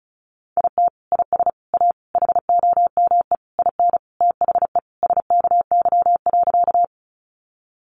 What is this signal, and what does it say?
Morse code, heard as a keyed tone, about 34 wpm, 700 Hz. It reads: IT IS A HOME IN THE SKY.
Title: Sentences from Top 1200 Words 35wpm